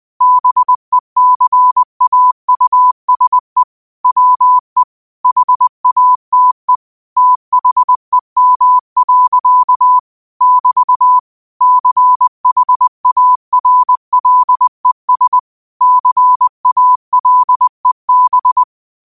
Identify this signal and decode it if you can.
Morse code, heard as a keyed tone, about 20 wpm, 1000 Hz. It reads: BECAUSE WE HATE THEM. <BT> CHARLES CALEB
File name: qotdfiles / 20wpm